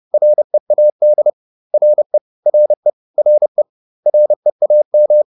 Morse code, heard as a keyed tone, about 30 wpm, 600 Hz. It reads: READ RE RE RE REAM
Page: advanced-practice-files